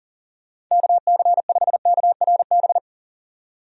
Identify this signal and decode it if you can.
Morse code, heard as a keyed tone, about 39 wpm, 700 Hz. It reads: KC5KRB
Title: USA Call Signs - 6 Characters - 40wpm